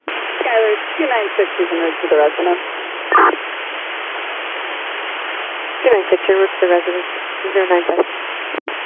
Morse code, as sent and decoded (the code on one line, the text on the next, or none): none